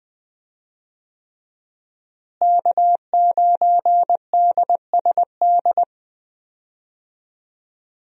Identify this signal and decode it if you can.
Morse code, heard as a keyed tone, about 21 wpm, 700 Hz. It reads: K9DSD
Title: USA Call Signs - 5 Characters - Repeated 3x - 20wpm